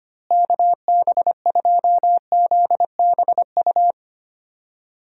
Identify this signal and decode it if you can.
Morse code, heard as a keyed tone, about 25 wpm, 700 Hz. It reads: KB2ZBU
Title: USA Call Signs - 6 Characters - Repeated 3x - 25wpm